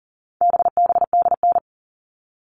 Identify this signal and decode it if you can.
Morse code, heard as a keyed tone, about 40 wpm, 700 Hz. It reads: BBDN
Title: All Cut Numbers - 4-Digit 40wpm